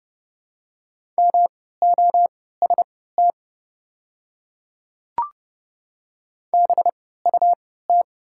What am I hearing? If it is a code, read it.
Morse code, heard as a keyed tone, about 30 wpm, spaced as for 17 wpm, 700 Hz. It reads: MOST BUT
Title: 3x Character Spacing - Top 100 Words - 30wpm